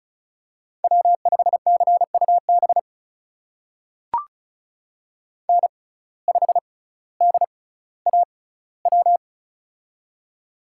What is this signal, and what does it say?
Morse code, heard as a keyed tone, about 36 wpm, spaced as for 18 wpm, 700 Hz. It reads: W5CUB N5DAW
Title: USA Call Signs - 5 Characters - 6x Character Spacing - 35wpm